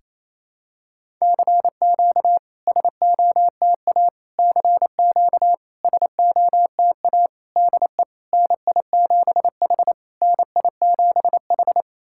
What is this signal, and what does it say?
Morse code, heard as a keyed tone, about 28 wpm, 700 Hz. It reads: CQ SOTA CQ SOTA DE NI7H NI7H
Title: SOTA QSO Practice 28wpm